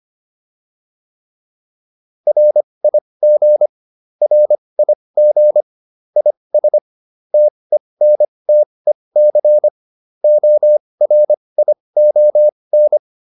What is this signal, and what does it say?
Morse code, heard as a keyed tone, about 25 wpm, spaced as for 20 wpm, 600 Hz. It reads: RIG RIG IS TENTEC ORION
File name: QSO402 20